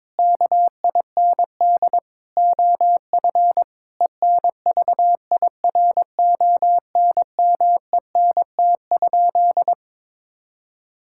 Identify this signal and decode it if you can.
Morse code, heard as a keyed tone, about 22 wpm, 700 Hz. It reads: KIND OF ENVIRONMENT?
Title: Sentences from Top 1000 Words - Review Part 5 - 22wpm